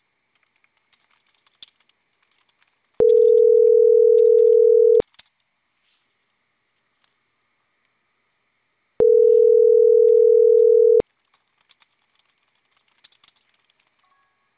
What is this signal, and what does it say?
Ringback tone (North America), 440 + 480 Hz.